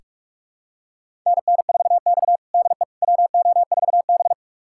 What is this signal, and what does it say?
Morse code, heard as a keyed tone, about 44 wpm, 690 Hz. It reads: NN4X DE WO4B
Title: USA Callsign DE Callsign - 4 Characters - 45wpm